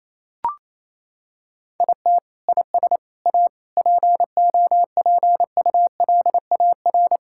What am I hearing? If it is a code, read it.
Morse code, heard as a keyed tone, about 28 wpm, 700 Hz. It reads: IT IS A POPULAR
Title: Sentences from Top 1000 Words - Review Part 5 - 28wpm